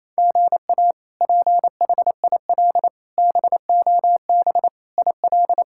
Morse code, heard as a keyed tone, about 28 wpm, 700 Hz. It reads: GA PHIL BOB IL
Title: K1USN SST QSO Practice - 28wpm